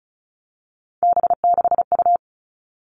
Morse code, heard as a keyed tone, about 36 wpm, 700 Hz. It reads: B6U
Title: Three Challenging Letter-Number Combinations 35wpm